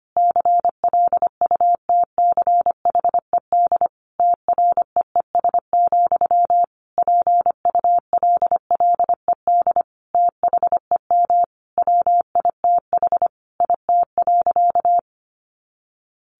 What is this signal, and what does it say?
Morse code, heard as a keyed tone, about 25 wpm, 700 Hz. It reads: CLUTCHED TREES, PULLED THEM WITH IT.
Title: A Sound Of Thunder 25wpm